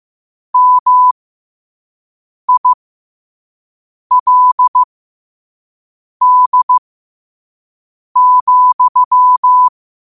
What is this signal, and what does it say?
Morse code, heard as a keyed tone, about 15 wpm, spaced as for 5 wpm, 1000 Hz. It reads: MILD,
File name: qotdfiles / 5wpm